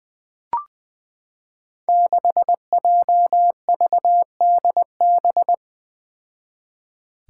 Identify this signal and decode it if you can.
Morse code, heard as a keyed tone, about 20 wpm, 700 Hz. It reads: 6JVDB
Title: Five Challenging Letter-Number Combinations 20wpm